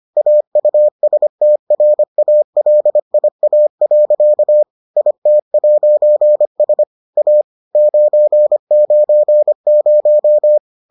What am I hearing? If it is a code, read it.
Morse code, heard as a keyed tone, about 25 wpm, 600 Hz. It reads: AUSTRALIA. IT'S A 990